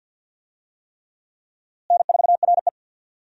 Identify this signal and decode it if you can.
Morse code, heard as a keyed tone, about 49 wpm, 700 Hz. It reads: N4RE
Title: USA Call Signs - 4 Characters - Repeated 3x - 50wpm